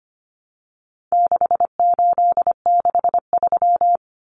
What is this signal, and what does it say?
Morse code, heard as a keyed tone, about 25 wpm, 700 Hz. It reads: 6863
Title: Common Cut Numbers - 4-Digit 25wpm